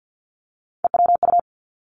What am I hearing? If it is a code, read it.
Morse code, heard as a keyed tone, about 50 wpm, 700 Hz. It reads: EWU
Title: Three Challenging Letter-Number Combinations 50wpm